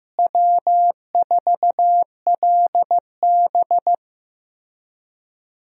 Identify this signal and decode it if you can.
Morse code, heard as a keyed tone, about 15 wpm, 700 Hz. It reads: W4LB
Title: USA Call Signs - 4 Characters - Repeated 2x - 15wpm